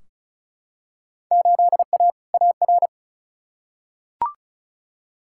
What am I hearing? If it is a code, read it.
Morse code, heard as a keyed tone, about 36 wpm, 700 Hz. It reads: 8A AR
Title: Field Day Exchanges - 35wpm